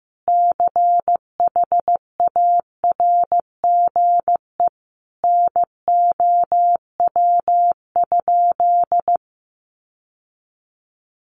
Morse code, heard as a keyed tone, about 15 wpm, 700 Hz. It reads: CHARGE NOW?
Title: Sentences from Top 1200 Words 15wpm